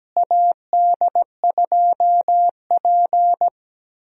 Morse code, heard as a keyed tone, about 17 wpm, 700 Hz. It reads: AD2P
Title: USA Call Signs - 4 Characters - Repeated 3x - 17wpm